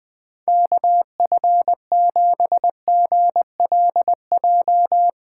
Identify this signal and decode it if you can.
Morse code, heard as a keyed tone, about 20 wpm, 700 Hz. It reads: KF7GLJ
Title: USA Callsign DE Callsign - 6 Characters - 20wpm